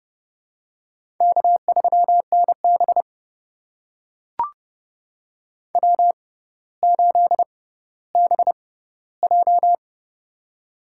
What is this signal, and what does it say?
Morse code, heard as a keyed tone, about 30 wpm, spaced as for 15 wpm, 700 Hz. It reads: K3NB W8BJ
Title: USA Call Signs - 4 Characters - 6x Character Spacing - 30wpm